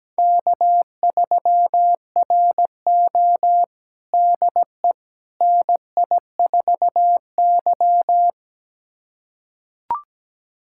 Morse code, heard as a keyed tone, about 17 wpm, 700 Hz. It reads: K3RO DE NI4Y
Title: USA Callsign DE Callsign - 4 Characters - 17wpm